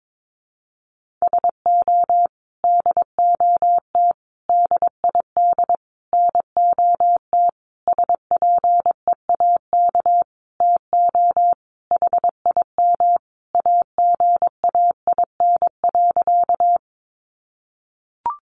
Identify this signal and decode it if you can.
Morse code, heard as a keyed tone, about 22 wpm, 700 Hz. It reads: SO DOT DID NOT SPEAK TO HIM AGAIN.
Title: Sentences from Top 1500 Words 22wpm